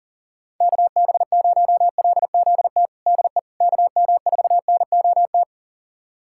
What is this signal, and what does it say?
Morse code, heard as a keyed tone, about 39 wpm, 690 Hz. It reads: KB0LZT DE KM4NOT
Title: USA Callsign DE Callsign - 6 Characters - 40wpm